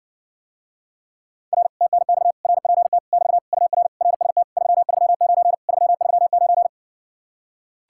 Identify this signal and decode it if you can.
Morse code, heard as a keyed tone, about 60 wpm, 700 Hz. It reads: R TNX RPT <BT> UR RST 339 339